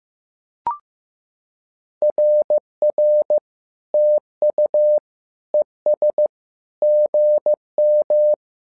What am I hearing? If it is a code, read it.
Morse code, heard as a keyed tone, about 15 wpm, 600 Hz. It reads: RR TU ES GM
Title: SOTA QSO Practice 15wpm